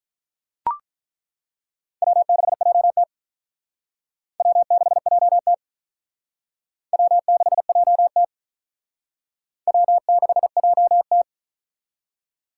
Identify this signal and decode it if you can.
Morse code, heard as a keyed tone, about 42 wpm, 700 Hz. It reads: W6JT W6JT W6JT W6JT
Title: Speed Racing - USA 1x2 Call Signs 35wpm